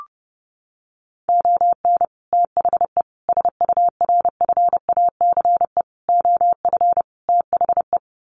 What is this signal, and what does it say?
Morse code, heard as a keyed tone, about 30 wpm, 700 Hz. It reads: ON THE SURFACE OF THE
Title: Sets of 5 Words Encore 30wpm